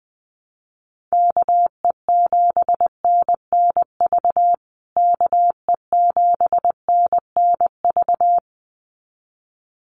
Morse code, heard as a keyed tone, about 20 wpm, 700 Hz. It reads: KE7NNV KE7NNV
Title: USA Call Signs Repeated 2x 20wpm